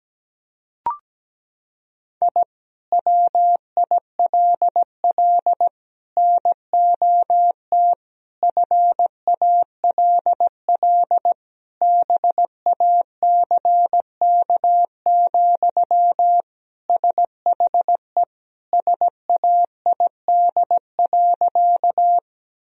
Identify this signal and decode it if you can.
Morse code, heard as a keyed tone, about 17 wpm, 700 Hz. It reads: I WILL NOT FALL BACK, SHE SAID.
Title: Sentences from Top 1000 Words - Review Part 2 - 17wpm